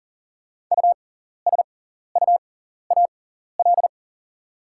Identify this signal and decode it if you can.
Morse code, heard as a keyed tone, about 40 wpm, spaced as for 14 wpm, 700 Hz. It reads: USUAL
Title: 6x Character Spacing - Top 500 Words - 40wpm